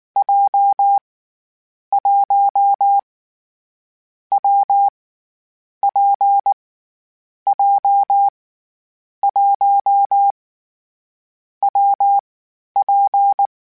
Morse code, heard as a keyed tone, about 19 wpm, 800 Hz. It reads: J 1 W P J 1 W P
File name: Difficult Combos / Difficult Combos-mp3s / WPJ WPJ1 @18